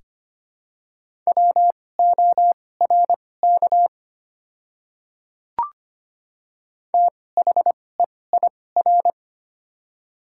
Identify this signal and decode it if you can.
Morse code, heard as a keyed tone, about 25 wpm, spaced as for 18 wpm, 700 Hz. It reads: WORK THEIR
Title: 2x Character Spacing - Top 200 Words - 25wpm